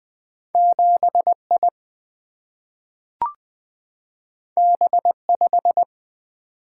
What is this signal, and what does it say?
Morse code, heard as a keyed tone, about 20 wpm, 700 Hz. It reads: 7I B5
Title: Single-Letter Single-Number 20wpm